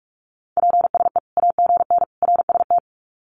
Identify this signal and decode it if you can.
Morse code, heard as a keyed tone, about 44 wpm, 700 Hz. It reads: PSE AGN RST
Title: Common QSO Phrases 45wpm